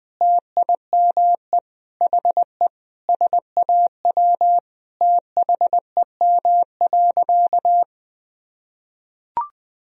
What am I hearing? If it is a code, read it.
Morse code, heard as a keyed tone, about 19 wpm, 700 Hz. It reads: TIME HE SAW THEM.
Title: Sentences from Top 1000 Words - Review Part 1 - 20wpm